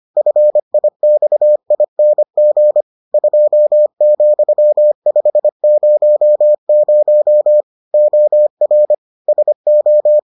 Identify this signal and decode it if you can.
Morse code, heard as a keyed tone, about 25 wpm, 600 Hz. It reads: FIXING 2,500 OR SO